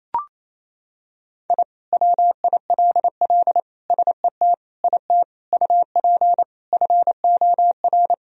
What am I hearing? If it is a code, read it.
Morse code, heard as a keyed tone, about 28 wpm, 700 Hz. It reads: I WILL SET IT UP FOR
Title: Sentences from Top 1000 Words - Review Part 1 - 28wpm